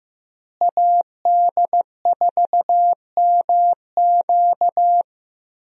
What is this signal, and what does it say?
Morse code, heard as a keyed tone, about 15 wpm, 700 Hz. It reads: AD4MQ
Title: USA Call Signs - 5 Characters - Repeated 3x - 15wpm